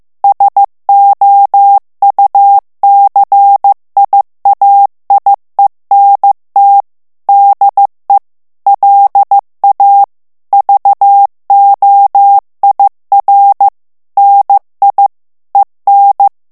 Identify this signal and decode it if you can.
Morse code, heard as a keyed tone, about 15 wpm, 800 Hz. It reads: SOUCIAIENT DE LA VOIR NI EN
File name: FRANCAIS TEXTE CLAIR 2 VITESSE 14 MOTS MINUTES